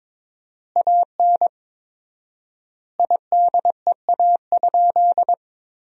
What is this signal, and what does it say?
Morse code, heard as a keyed tone, about 22 wpm, 700 Hz. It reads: AN IDEA?
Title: 4x Word-Spacing - Sentences from Top 400 Words - 22wpm